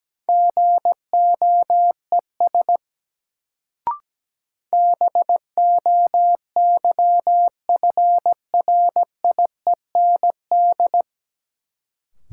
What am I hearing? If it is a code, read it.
Morse code, heard as a keyed tone, about 17 wpm, 700 Hz. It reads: GOES BOYFRIEND
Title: Sentences from Top 1000 Words - Review Part 5 - New Word Drill - 17wpm